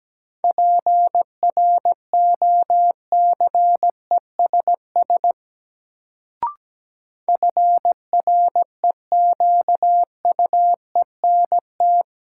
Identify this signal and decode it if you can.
Morse code, heard as a keyed tone, about 17 wpm, 700 Hz. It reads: PROCESS FREQUENT
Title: Sentences from Top 1400 Words - New Word Drill Alt - 17wpm